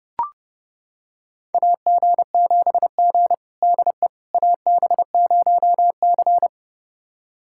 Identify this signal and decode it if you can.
Morse code, heard as a keyed tone, about 32 wpm, 700 Hz. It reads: AG7G DE AB0C
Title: USA Callsign DE Callsign - 4 Characters - 30wpm